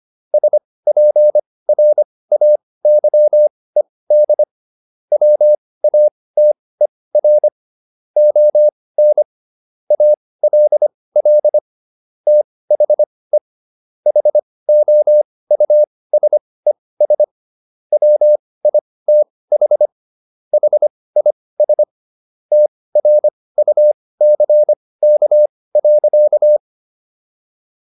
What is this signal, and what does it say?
Morse code, heard as a keyed tone, about 25 wpm, spaced as for 18 wpm, 600 Hz. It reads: SPRAYED WATER ON ALL THE HOUSES WITH HIS TRUCK.